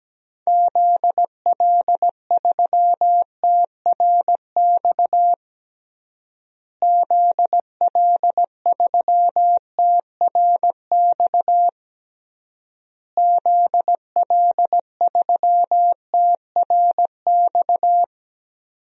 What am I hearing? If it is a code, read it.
Morse code, heard as a keyed tone, about 17 wpm, 700 Hz. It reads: ZL3TRX ZL3TRX ZL3TRX
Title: DXCC-100 Call Signs - 6 Characters - Repeated 3x - 17wpm